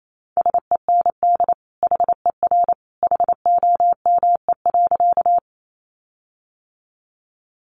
Morse code, heard as a keyed tone, about 28 wpm, 700 Hz. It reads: SEND HER HOME.
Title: Sentences from Top 1500 Words 28wpm